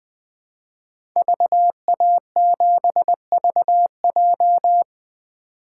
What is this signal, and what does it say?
Morse code, heard as a keyed tone, about 20 wpm, 700 Hz. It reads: VA7VJ
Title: 2023 SOTA Chaser Call Signs - 20wpm